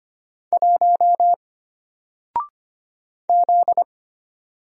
Morse code, heard as a keyed tone, about 25 wpm, 700 Hz. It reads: 1 Z
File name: Single-Letter-Number-25wpm